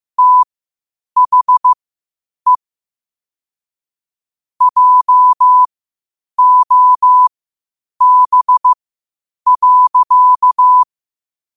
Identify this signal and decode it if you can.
Morse code, heard as a keyed tone, about 15 wpm, spaced as for 8 wpm, 1000 Hz. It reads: THE JOB.
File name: qotdfiles / 8wpm